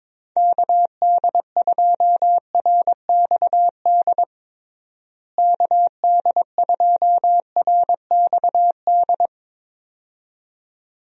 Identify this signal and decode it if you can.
Morse code, heard as a keyed tone, about 21 wpm, 700 Hz. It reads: KD2RXD KD2RXD
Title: USA Call Signs - 6 Characters - Repeated 2x - 22wpm